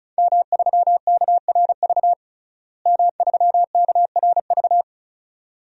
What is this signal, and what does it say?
Morse code, heard as a keyed tone, about 36 wpm, 700 Hz. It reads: M3KRV M3KRV
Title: DXCC-100 Call Signs - 5 Characters - Repeated 2x - 35wpm